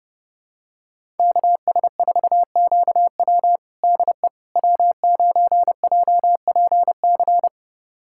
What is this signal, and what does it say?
Morse code, heard as a keyed tone, about 30 wpm, 700 Hz. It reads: KS4QW DE W9JPC
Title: USA Callsign DE Callsign - 5 Characters - 30wpm